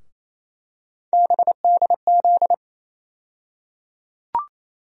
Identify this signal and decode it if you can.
Morse code, heard as a keyed tone, about 28 wpm, 700 Hz. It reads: BDZ
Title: USA Call Sign Postfixes - 3 Letters 28wpm